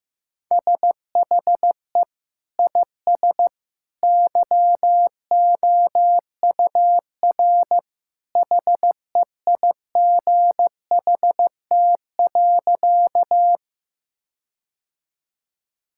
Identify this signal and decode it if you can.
Morse code, heard as a keyed tone, about 15 wpm, 700 Hz. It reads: SHE IS YOUR HEIGHT.
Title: Sentences from Top 1600 Words 15wpm